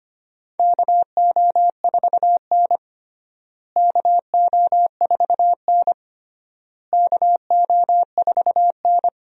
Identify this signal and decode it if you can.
Morse code, heard as a keyed tone, about 25 wpm, 700 Hz. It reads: KO4N KO4N KO4N
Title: USA Call Signs - 4 Characters - Repeated 3x - 25wpm